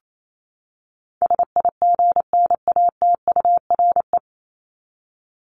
Morse code, heard as a keyed tone, about 28 wpm, 700 Hz. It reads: SIGNATURE
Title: Top 1751-2000 Words 28wpm